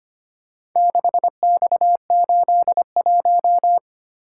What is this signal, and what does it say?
Morse code, heard as a keyed tone, about 25 wpm, 700 Hz. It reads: 6X81
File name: practice / 25wpm / Four Letter-Number Combinations - 25wpm